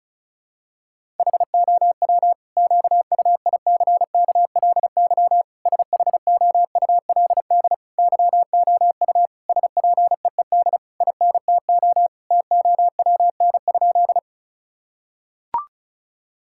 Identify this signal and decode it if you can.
Morse code, heard as a keyed tone, about 34 wpm, 700 Hz. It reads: HOW QUICKLY SHOULD YOU SPEED INTO TOWN?
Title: Sentences from Top 1000 Words - Review Part 5 - 35wpm